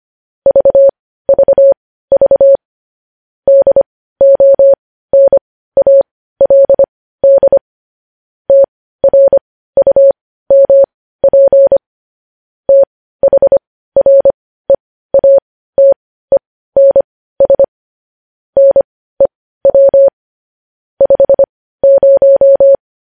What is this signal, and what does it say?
Morse code, heard as a keyed tone, about 25 wpm, spaced as for 15 wpm, 550 Hz. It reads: VVV DONALD TRUMP THREATENS NEW 50
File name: podcast / News headlines-1x-25x15